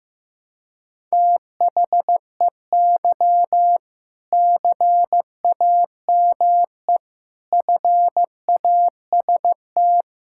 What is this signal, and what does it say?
Morse code, heard as a keyed tone, about 15 wpm, 700 Hz. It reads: THEY CAME FAST
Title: Sentences from Top 1600 Words 15wpm